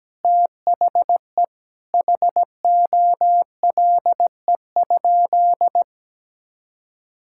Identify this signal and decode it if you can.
Morse code, heard as a keyed tone, about 17 wpm, 700 Hz. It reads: THE HOLE?
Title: Sentences from Top 1600 Words 17wpm